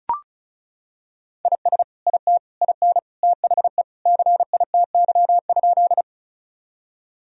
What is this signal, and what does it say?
Morse code, heard as a keyed tone, about 34 wpm, 700 Hz. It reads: IS IT IN THE CITY?
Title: Sentences from Top 1000 Words - Review Part 1 - 35wpm